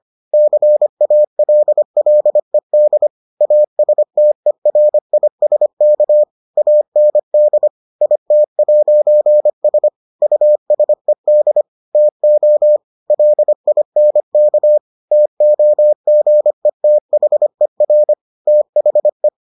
Morse code, heard as a keyed tone, about 25 wpm, 600 Hz. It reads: CALLED ASTERISK AND IT'S USED TO LINK TOGETHER THE